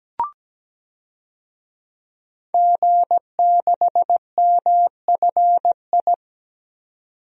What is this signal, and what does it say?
Morse code, heard as a keyed tone, about 17 wpm, 700 Hz. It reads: G6MFI
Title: DXCC-100 Call Signs - 5 Characters - 17wpm